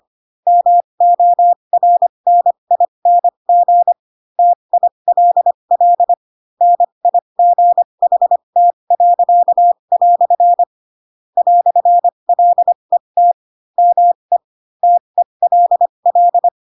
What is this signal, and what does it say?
Morse code, heard as a keyed tone, about 25 wpm, spaced as for 22 wpm, 700 Hz. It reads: MORNING TILL NIGHT." "LET ME TELL
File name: Pinocchio / Pinocchio ch4 18 25 ik1hge